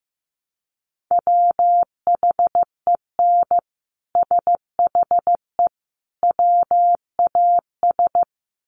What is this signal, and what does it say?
Morse code, heard as a keyed tone, about 15 wpm, 700 Hz. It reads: WHEN SHE WAS
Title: Sets of 3 Words 15wpm